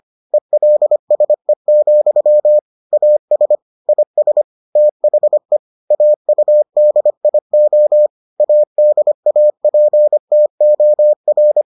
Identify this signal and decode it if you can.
Morse code, heard as a keyed tone, about 25 wpm, 600 Hz. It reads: ELSE, AS IS THE AUDIO ADAPTOR